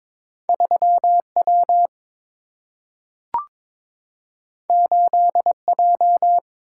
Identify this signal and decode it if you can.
Morse code, heard as a keyed tone, about 22 wpm, 700 Hz. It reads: 3 W 8 J
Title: Single-Letter Single-Number 22wpm